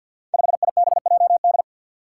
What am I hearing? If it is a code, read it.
Morse code, heard as a keyed tone, about 52 wpm, 700 Hz. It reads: 5IBJD
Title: Five Challenging Letter-Number Combinations 50wpm